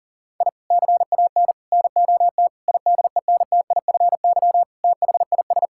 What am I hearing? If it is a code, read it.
Morse code, heard as a keyed tone, about 40 wpm, 700 Hz. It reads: I CAN NOT IDENTIFY THIS
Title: Sentences from Top 1000 Words - Review Part 6 - 40wpm